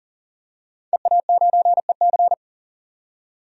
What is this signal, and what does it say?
Morse code, heard as a keyed tone, about 40 wpm, 700 Hz. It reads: EA9EC
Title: DXCC-100 Call Signs - 5 Characters - 40wpm